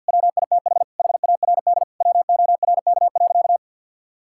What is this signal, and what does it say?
Morse code, heard as a keyed tone, about 52 wpm, 700 Hz. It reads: WITH HARD WORK.
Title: Sentences from Top 1000 Words - Review Part 5 - 50wpm